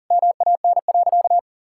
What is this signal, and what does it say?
Morse code, heard as a keyed tone, about 37 wpm, 700 Hz. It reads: MAN.
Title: Sentences from Top 1000 Words - Review Part 5 - 40wpm